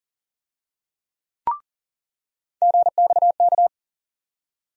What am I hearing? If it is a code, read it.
Morse code, heard as a keyed tone, about 40 wpm, 700 Hz. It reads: GXK
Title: USA Call Sign Postfixes - 3 Letters 40wpm